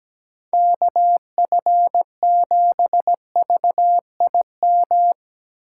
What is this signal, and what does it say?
Morse code, heard as a keyed tone, about 17 wpm, 700 Hz. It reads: KF7VIM
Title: USA Call Signs - 6 Characters - 2x Character Spacing - 17wpm